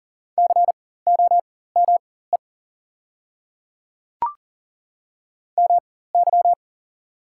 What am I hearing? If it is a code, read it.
Morse code, heard as a keyed tone, about 40 wpm, spaced as for 19 wpm, 690 Hz. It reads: COME MY
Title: 4x Character Spacing - Top 100 Words - 40wpm